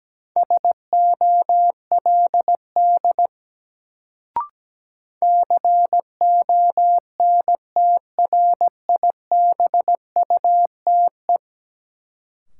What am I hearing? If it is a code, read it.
Morse code, heard as a keyed tone, about 17 wpm, 700 Hz. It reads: SOLD CONTRIBUTE
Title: Sentences from Top 1400 Words - New Word Drill - 17wpm